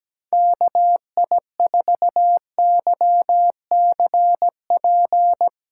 Morse code, heard as a keyed tone, about 17 wpm, 700 Hz. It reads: KI4YCP
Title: USA Callsign DE Callsign - 6 Characters - 17wpm